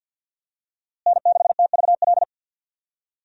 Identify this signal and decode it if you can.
Morse code, heard as a keyed tone, about 51 wpm, 700 Hz. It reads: N6TVL
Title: USA Call Signs - 5 Characters - Repeated 2x - 50wpm